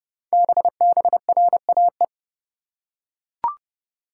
Morse code, heard as a keyed tone, about 29 wpm, 700 Hz. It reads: BBRAE
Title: Code Groups - 5 Letters - 30wpm